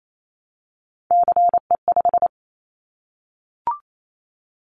Morse code, heard as a keyed tone, about 28 wpm, 700 Hz. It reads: CE5
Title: Callsign Prefixes for Top 100 DXCC - 28wpm